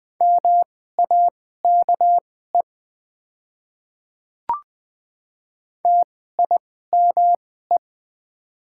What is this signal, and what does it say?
Morse code, heard as a keyed tone, about 21 wpm, spaced as for 15 wpm, 690 Hz. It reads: MAKE TIME
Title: 2x Character Spacing - Top 200 Words - 20wpm